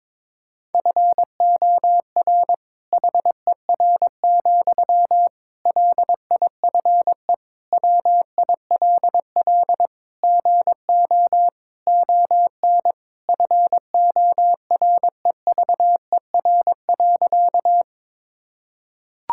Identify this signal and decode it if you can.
Morse code, heard as a keyed tone, about 23 wpm, 700 Hz. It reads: FOR HER, LIFE WILL GO ON FOREVER.
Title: Sentences from Top 1600 Words 22wpm